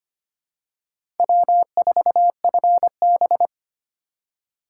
Morse code, heard as a keyed tone, about 25 wpm, 700 Hz. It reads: W4FB
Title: USA Call Signs - 4 Characters - 8x Character Spacing - 25wpm